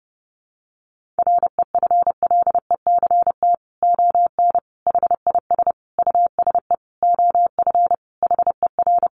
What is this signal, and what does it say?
Morse code, heard as a keyed tone, about 30 wpm, 700 Hz. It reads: REFLECT ON HIS USE OF HER
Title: Sentences from Top 1500 Words 30wpm